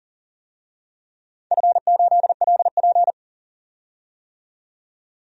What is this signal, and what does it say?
Morse code, heard as a keyed tone, about 37 wpm, 700 Hz. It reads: F8LP
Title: DXCC-100 Call Signs - 4 Characters - 40wpm